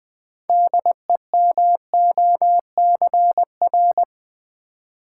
Morse code, heard as a keyed tone, about 20 wpm, 700 Hz. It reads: DEMOCR
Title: Build A Word - 3 Letters At A Time - 9-Letter Words - 20wpm